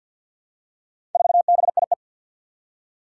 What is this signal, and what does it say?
Morse code, heard as a keyed tone, about 49 wpm, 700 Hz. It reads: 4BIE
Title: Four Challenging Letter-Number Combinations 50wpm